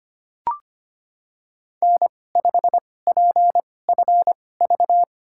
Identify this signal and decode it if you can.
Morse code, heard as a keyed tone, about 25 wpm, spaced as for 18 wpm, 700 Hz. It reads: N5PFV
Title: USA Call Signs - 2x Character Spacing - 25wpm